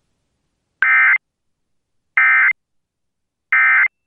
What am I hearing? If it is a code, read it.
Emergency Alert System (SAME) tones, heard as harsh buzzing bursts. It is heard from 0.8 to 3.9 s.